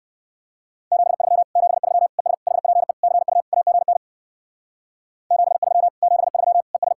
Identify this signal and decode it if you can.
Morse code, heard as a keyed tone, about 57 wpm, 700 Hz. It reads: 73 73 ES HPE CU AGN 73 73 ES